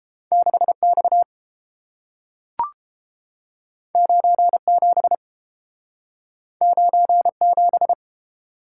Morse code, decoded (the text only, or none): 6X 97 97